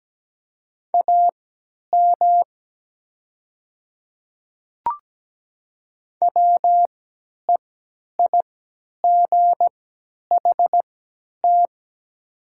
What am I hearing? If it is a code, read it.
Morse code, heard as a keyed tone, about 17 wpm, spaced as for 10 wpm, 700 Hz. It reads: AM WEIGHT
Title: 3x Character Spacing - Top 500 Words - 17wpm